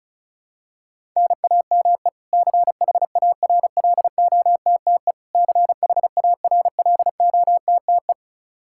Morse code, heard as a keyed tone, about 34 wpm, 700 Hz. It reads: NAME CHARLOTTE CHARLOTTE
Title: Practice QSO - Exchange 1 - Beginning - 35wpm